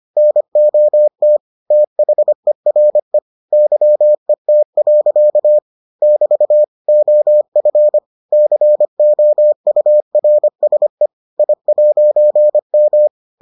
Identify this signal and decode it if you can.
Morse code, heard as a keyed tone, about 25 wpm, 600 Hz. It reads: NOT THERE YET. <BT> OF COURSE I'M